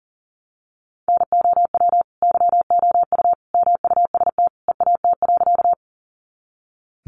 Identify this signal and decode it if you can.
Morse code, heard as a keyed tone, about 40 wpm, 700 Hz. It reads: NOW YOU MUST EAT.
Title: Sentences from Top 300 Words 40wpm